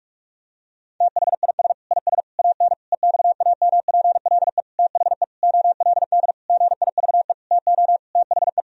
Morse code, heard as a keyed tone, about 46 wpm, 700 Hz. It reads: THIS IS AN EXAMPLE THE OLD GIVE TO THE
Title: Sentences from Top 1000 Words - Review Part 1 - 45wpm